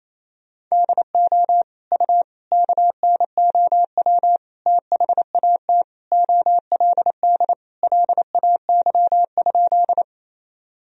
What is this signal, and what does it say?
Morse code, heard as a keyed tone, about 28 wpm, 700 Hz. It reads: DO U KNOW THAT OLD LAY?
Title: Sentences from Top 400 Words and Abbreviations 28wpm